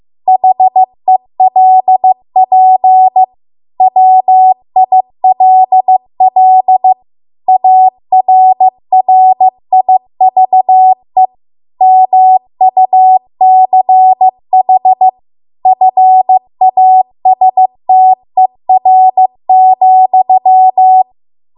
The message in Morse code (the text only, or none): HELP WILL ARRIVE MUCH FASTER,